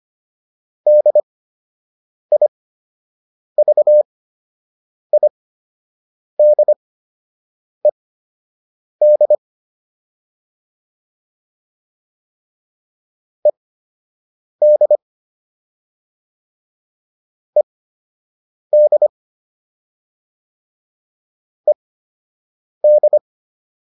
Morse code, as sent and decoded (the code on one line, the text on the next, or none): -.. .. ...- .. -.. -.. / . -.. / . -.. / . -..
DIVIDED ED ED ED